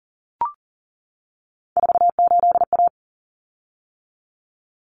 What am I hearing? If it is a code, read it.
Morse code, heard as a keyed tone, about 40 wpm, 700 Hz. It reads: 48A
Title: Common Cut Numbers - 3-Digit 40wpm